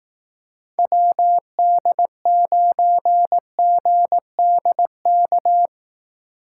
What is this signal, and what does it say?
Morse code, heard as a keyed tone, about 18 wpm, 700 Hz. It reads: WD9GDK
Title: Speed Racing - USA 2x3 Call Signs 15wpm